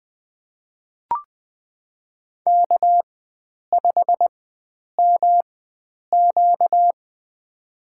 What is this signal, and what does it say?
Morse code, heard as a keyed tone, about 20 wpm, spaced as for 9 wpm, 700 Hz. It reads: K5MQ
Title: USA Call Signs - 4x Character Spacing - 20wpm